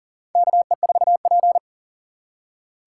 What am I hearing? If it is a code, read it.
Morse code, heard as a keyed tone, about 40 wpm, 700 Hz. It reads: KE4P
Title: USA Call Signs - 4 Characters - 2x Character Spacing - 40wpm